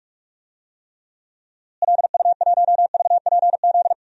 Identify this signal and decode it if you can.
Morse code, heard as a keyed tone, about 46 wpm, 700 Hz. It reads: LU1VPZ
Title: DXCC-100 Call Signs - 6 Characters - Repeated 2x - 45wpm